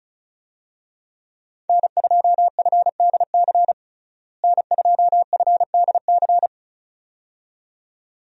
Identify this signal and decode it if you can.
Morse code, heard as a keyed tone, about 35 wpm, 700 Hz. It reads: N2FDC N2FDC
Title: USA Call Signs - 5 Characters - Repeated 2x - 35wpm